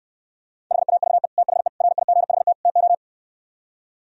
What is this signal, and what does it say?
Morse code, heard as a keyed tone, about 68 wpm, 700 Hz. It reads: HAVE THE RIGHT TO